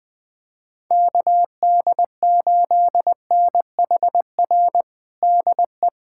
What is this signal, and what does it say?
Morse code, heard as a keyed tone, about 21 wpm, 700 Hz. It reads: KD8NHR DE
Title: USA Callsign DE Callsign 20wpm